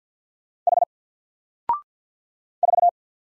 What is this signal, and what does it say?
Morse code, heard as a keyed tone, about 50 wpm, 700 Hz. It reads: H 4
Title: Single Letter-Number Alt 50wpm